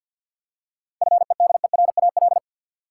Morse code, heard as a keyed tone, about 52 wpm, 700 Hz. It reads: FEDERAL
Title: Seven Letter Words - 50wpm